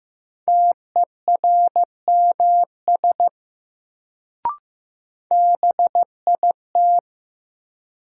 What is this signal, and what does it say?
Morse code, heard as a keyed tone, about 15 wpm, 700 Hz. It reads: TERMS BIT
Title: Sets of 3 Words - New Word Drill - 15wpm